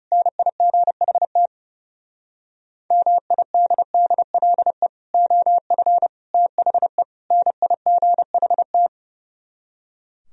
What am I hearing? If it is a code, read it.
Morse code, heard as a keyed tone, about 38 wpm, 700 Hz. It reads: NIGHT MIDDLE OF THE NIGHT